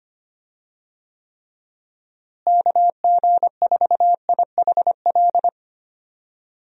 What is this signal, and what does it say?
Morse code, heard as a keyed tone, about 25 wpm, 700 Hz. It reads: KG4IHL
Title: USA Call Signs - 6 Characters - Repeated 2x - 25wpm